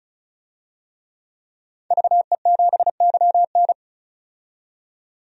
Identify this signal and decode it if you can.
Morse code, heard as a keyed tone, about 36 wpm, 700 Hz. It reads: VE7YN
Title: DXCC-100 Call Signs - 5 Characters - 35wpm